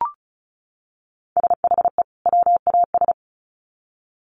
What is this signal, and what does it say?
Morse code, heard as a keyed tone, about 34 wpm, 700 Hz. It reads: SHE WAS
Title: Sets of 2 Words 35wpm